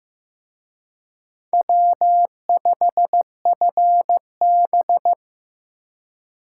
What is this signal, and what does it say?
Morse code, heard as a keyed tone, about 18 wpm, 700 Hz. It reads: W5FB
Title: Speed Racing - USA 1x2 Call Signs 15wpm